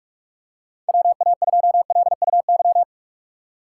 Morse code, heard as a keyed tone, about 46 wpm, 700 Hz. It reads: WA2LUY
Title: USA Call Signs - 6 Characters - 3x Character Spacing - 45wpm